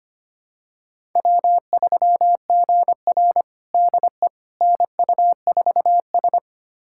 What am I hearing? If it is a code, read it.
Morse code, heard as a keyed tone, about 25 wpm, 700 Hz. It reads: W3GR DE NU4S
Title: USA Callsign DE Callsign - 4 Characters - 25wpm